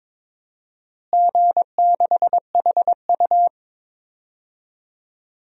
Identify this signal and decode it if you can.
Morse code, heard as a keyed tone, about 22 wpm, 700 Hz. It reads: G6HU
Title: DXCC-100 Call Signs - 4 Characters - Repeated 2x - 22wpm